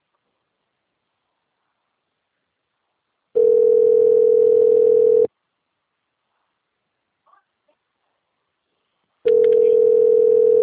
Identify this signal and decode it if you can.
Ringback tone (North America), 440 + 480 Hz.